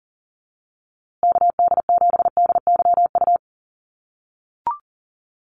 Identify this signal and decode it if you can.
Morse code, heard as a keyed tone, about 40 wpm, 700 Hz. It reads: KD7DYU
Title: USA Call Signs Repeated 3x 40wpm